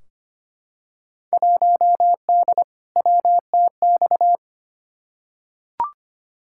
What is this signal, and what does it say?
Morse code, heard as a keyed tone, about 25 wpm, 700 Hz. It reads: 1D WTX
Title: Field Day Exchanges - 25wpm